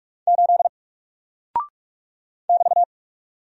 Morse code, heard as a keyed tone, about 44 wpm, 700 Hz. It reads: 8 <BT>
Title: Single Character (Letter:Number:Punctuation:Prosign) Alt 45wpm